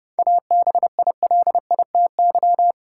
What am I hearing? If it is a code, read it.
Morse code, heard as a keyed tone, about 29 wpm, 700 Hz. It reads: ABILITY
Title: Sentences from Top 1000 Words - Review Part 3 - New Word Drill - 30wpm